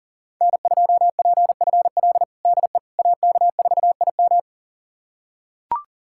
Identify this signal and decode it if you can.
Morse code, heard as a keyed tone, about 40 wpm, 700 Hz. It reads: N2PFL DE AK4IM
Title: USA Callsign DE Callsign - 5 Characters - 40wpm